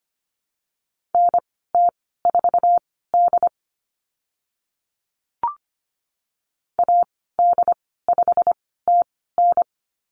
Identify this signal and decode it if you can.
Morse code, heard as a keyed tone, about 25 wpm, spaced as for 16 wpm, 700 Hz. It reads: NT4D AD5TN
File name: Lesson 15g - Number 5 - Call Signs 25wpm